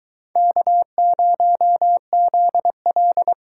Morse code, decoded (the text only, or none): K0ZL